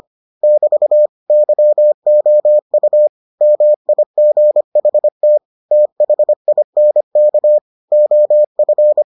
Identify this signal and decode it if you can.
Morse code, heard as a keyed tone, about 25 wpm, 600 Hz. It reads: <BT> YOU MIGHT THINK OF